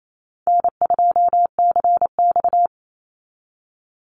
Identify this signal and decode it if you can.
Morse code, heard as a keyed tone, about 28 wpm, 700 Hz. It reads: N2CX
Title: Call Signs - 28wpm